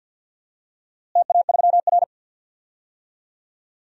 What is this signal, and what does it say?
Morse code, heard as a keyed tone, about 52 wpm, 700 Hz. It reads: TA3R
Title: DXCC-100 Call Signs - 4 Characters - Repeated 2x - 50wpm